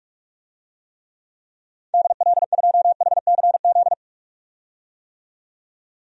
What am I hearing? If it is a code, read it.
Morse code, heard as a keyed tone, about 43 wpm, 700 Hz. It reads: DL2HCZ
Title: DXCC-100 Call Signs - 6 Characters - 45wpm